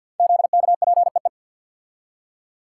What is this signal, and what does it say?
Morse code, heard as a keyed tone, about 52 wpm, 700 Hz. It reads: ZKPEE
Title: Code Groups - 5 Letters - 50wpm